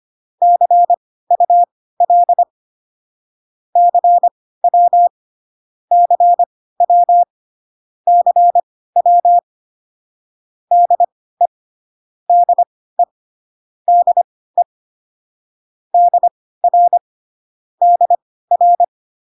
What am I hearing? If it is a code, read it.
Morse code, heard as a keyed tone, about 25 wpm, spaced as for 16 wpm, 700 Hz. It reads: CUL CW CW CW DE DE DE DR DR